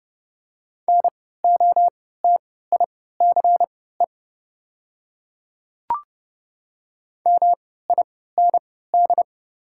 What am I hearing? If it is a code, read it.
Morse code, heard as a keyed tone, about 30 wpm, spaced as for 17 wpm, 690 Hz. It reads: NOTICE MIND
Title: 3x Character Spacing - Top 500 Words - 30wpm